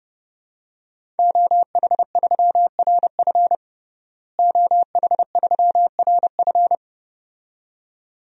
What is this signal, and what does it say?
Morse code, heard as a keyed tone, about 32 wpm, 700 Hz. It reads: OH3RF OH3RF
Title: DXCC-100 Call Signs - 5 Characters - Repeated 2x - 30wpm